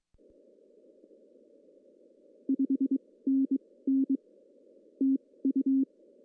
Morse code, heard as a keyed tone, about 22 wpm, spaced as for 17 wpm, 270 Hz. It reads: 5NN TU
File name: CW / cqww-cw-k5tr